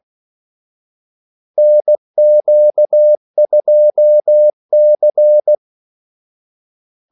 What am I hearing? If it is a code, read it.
Morse code, heard as a keyed tone, about 16 wpm, 600 Hz. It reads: NQ2C
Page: Kaldesignaler 16 wpm